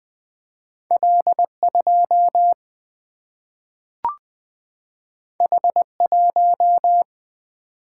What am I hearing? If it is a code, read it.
Morse code, heard as a keyed tone, about 20 wpm, 700 Hz. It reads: L2 H1
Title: Single-Letter Single-Number 20wpm